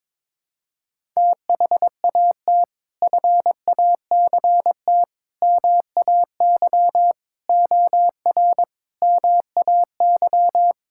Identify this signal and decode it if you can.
Morse code, heard as a keyed tone, about 22 wpm, 700 Hz. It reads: THAT FACT MAY OR MAY
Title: Sentences from Top 1000 Words - Review Part 6 - 22wpm